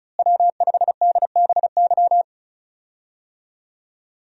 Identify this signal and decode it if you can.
Morse code, heard as a keyed tone, about 36 wpm, 700 Hz. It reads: W5DBY
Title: USA Call Signs - 5 Characters - Repeated 2x - 35wpm